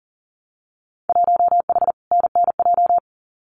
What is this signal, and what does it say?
Morse code, heard as a keyed tone, about 41 wpm, 700 Hz. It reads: 1H NNJ
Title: Winter Field Day Exchanges 40wpm